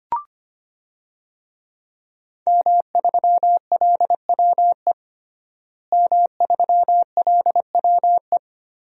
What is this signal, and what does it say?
Morse code, heard as a keyed tone, about 25 wpm, 700 Hz. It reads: M3LWE M3LWE
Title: DXCC-100 Call Signs - 5 Characters - Repeated 2x - 25wpm